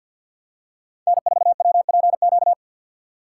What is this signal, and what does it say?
Morse code, heard as a keyed tone, about 48 wpm, 700 Hz. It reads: N4WPQ
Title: USA Call Signs - 5 Characters - 2x Character Spacing - 50wpm